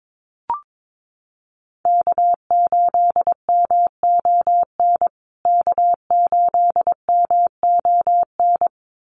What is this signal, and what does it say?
Morse code, heard as a keyed tone, about 22 wpm, 700 Hz. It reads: K8MON K8MON
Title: USA Call Signs Repeated 2x 22wpm